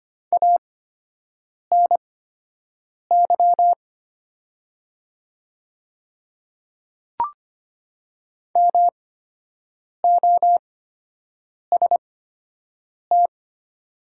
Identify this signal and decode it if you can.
Morse code, heard as a keyed tone, about 25 wpm, spaced as for 7 wpm, 700 Hz. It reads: ANY MOST
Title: 8x Character Spacing - Top 200 Words - 25wpm